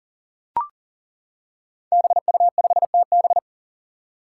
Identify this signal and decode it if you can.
Morse code, heard as a keyed tone, about 40 wpm, 700 Hz. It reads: BU5TB
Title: Five Challenging Letter-Number Combinations 40wpm